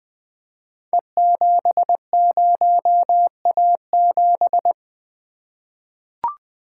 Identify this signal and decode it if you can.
Morse code, heard as a keyed tone, about 20 wpm, 700 Hz. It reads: E70A7
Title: Code Groups - 5 Letter:Numbers - 20wpm